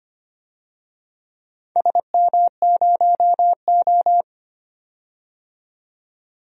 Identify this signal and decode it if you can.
Morse code, heard as a keyed tone, about 25 wpm, 690 Hz. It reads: SM0O
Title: DXCC-100 Call Signs - 4 Characters - 25wpm